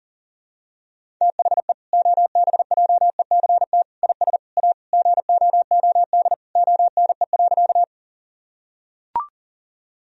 Morse code, heard as a keyed tone, about 40 wpm, 700 Hz. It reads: THE OBJECT IS A GOOD ONE.
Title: Sentences from Top 1000 Words - Review Part 3 - 40wpm